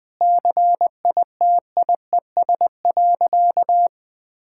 Morse code, heard as a keyed tone, about 21 wpm, 700 Hz. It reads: CITIES.
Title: Sentences with 5-Word Phrases - 20wpm